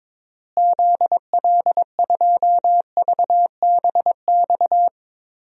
Morse code, heard as a keyed tone, about 22 wpm, 700 Hz. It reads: ZL2VBX
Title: DXCC-100 Call Signs - 6 Characters - Repeated 2x - 22wpm